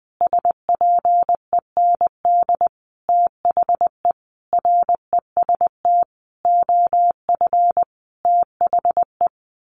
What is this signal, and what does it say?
Morse code, heard as a keyed tone, about 20 wpm, 700 Hz. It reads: SPEND THE REST OF THE
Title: Sentences from Top 1400 Words 20wpm